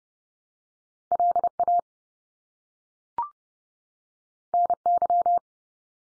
Morse code, heard as a keyed tone, about 30 wpm, 700 Hz. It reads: LA NY
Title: US States 30WPM